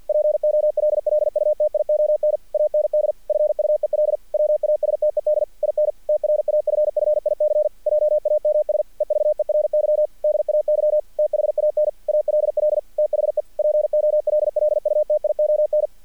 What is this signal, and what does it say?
Morse code, heard as a keyed tone, about 48 wpm, 600 Hz. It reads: POLLUTION AND FUEL WASTED IN TRAFFIC JAMS EVERY DAY THAN ALL THE POLLUTION
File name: archived / CRSnet